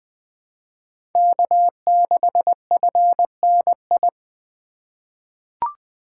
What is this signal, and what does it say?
Morse code, heard as a keyed tone, about 21 wpm, 700 Hz. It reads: K6FNI
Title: USA Call Signs - 5 Characters - 2x Character Spacing - 20wpm